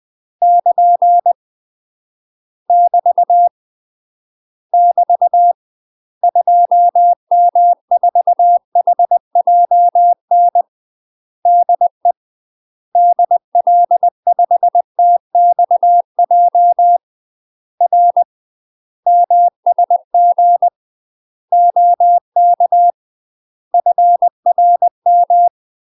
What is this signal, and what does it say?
Morse code, heard as a keyed tone, about 20 wpm, 700 Hz. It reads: <KN> <BT> <BT> 2M4HJN DE DL5TXJ R MSG OK FRM
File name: UK MASTER QSOs TRAINING/20WPM